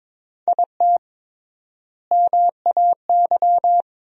Morse code, heard as a keyed tone, about 22 wpm, 700 Hz. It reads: IT MAY
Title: 3x Word-Spacing - Sentences from Top 400 Words - 22wpm